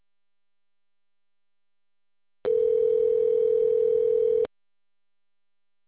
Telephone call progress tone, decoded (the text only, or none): ringback tone